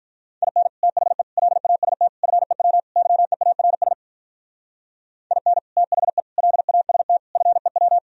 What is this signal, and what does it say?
Morse code, heard as a keyed tone, about 45 wpm, 700 Hz. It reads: IN THE LAST FEW YEARS IN THE LAST FEW